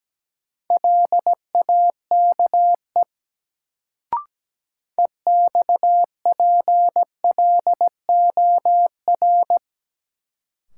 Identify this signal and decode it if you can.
Morse code, heard as a keyed tone, about 17 wpm, 700 Hz. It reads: LAKE EXPLOR
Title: Sentences from Top 1100 Words - New Word Drill Alt - 17wpm